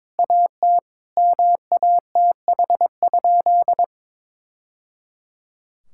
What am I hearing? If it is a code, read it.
Morse code, heard as a keyed tone, about 22 wpm, 700 Hz. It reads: AT MATH?
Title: Sentences from Top 1000 Words - Review Part 6 - 22wpm